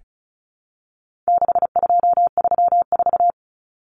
Morse code, heard as a keyed tone, about 34 wpm, 700 Hz. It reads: 6234
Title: Common Cut Numbers - 4-Digit 35wpm